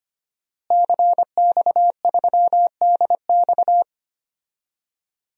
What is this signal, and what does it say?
Morse code, heard as a keyed tone, about 25 wpm, 700 Hz. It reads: CX3DX
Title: DXCC-100 Call Signs - 5 Characters - Repeated 3x - 25wpm